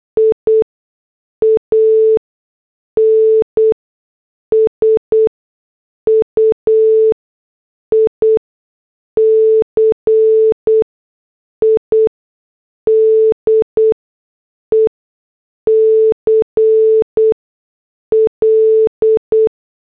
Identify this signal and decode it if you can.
Morse code, heard as a keyed tone, about 8 wpm, spaced as for 6 wpm, 440 Hz. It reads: IAN SUICIDE CL